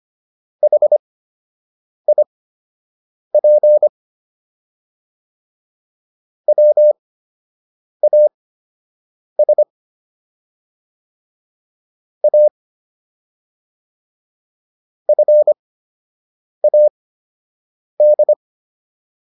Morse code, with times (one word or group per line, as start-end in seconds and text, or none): 0.6-3.9 HIP
6.5-9.6 WAS
12.2-12.5 A
15.1-18.3 FAD